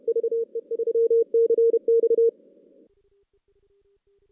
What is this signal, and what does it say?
Morse code, heard as a keyed tone, about 32 wpm, 460 Hz. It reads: VE3CX
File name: 2016 JAN CW NAQP / 2016 Jan naqp k5ot - 0-6